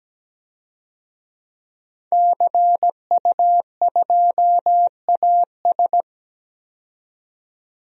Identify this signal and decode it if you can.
Morse code, heard as a keyed tone, about 17 wpm, 700 Hz. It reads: CU2AS
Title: DXCC-100 Call Signs - 5 Characters - 17wpm